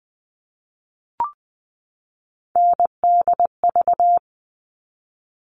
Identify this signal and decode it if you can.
Morse code, heard as a keyed tone, about 20 wpm, 700 Hz. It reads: NDV
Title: All Cut Numbers - 3-Digit 20wpm